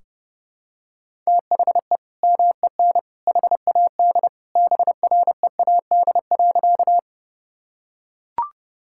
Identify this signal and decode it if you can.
Morse code, heard as a keyed tone, about 28 wpm, 700 Hz. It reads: THE MEN HAD BREAD.
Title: Sentences from Top 1600 Words 30wpm